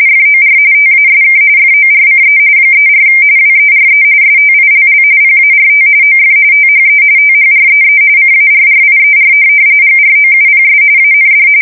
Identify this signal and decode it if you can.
RTTY, about 45.45 baud. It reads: FOR TASK FORCE 34 WERE STILL PART OF THE THREE CARRIER GROUPS AND
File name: fsk170 / USS-New-Jersey